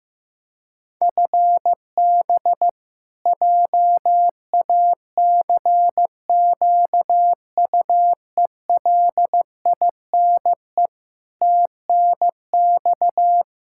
Morse code, heard as a keyed tone, about 15 wpm, 700 Hz. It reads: FB JACQUELINE TNX
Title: Practice QSO - Exchange 3 - More Info - 15wpm